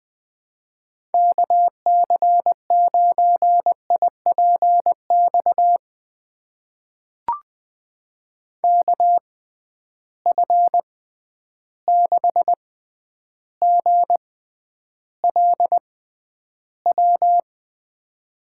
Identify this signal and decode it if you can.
Morse code, heard as a keyed tone, about 20 wpm, spaced as for 10 wpm, 700 Hz. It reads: KC9IPX KF6GLW
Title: USA Call Signs - 6 Characters - 6x Character Spacing - 20wpm